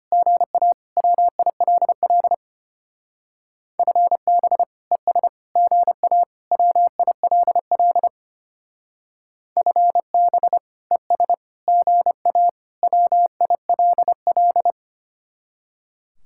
Morse code, heard as a keyed tone, about 32 wpm, 700 Hz. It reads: GA WILL FB ES GA WILL FB ES GA WILL